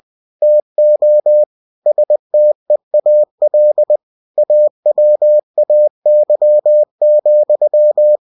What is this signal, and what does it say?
Morse code, heard as a keyed tone, about 20 wpm, 600 Hz. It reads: TO STEAL AWAY,
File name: cw-books / winnie-the-pooh / winnie-20wpm-page-0038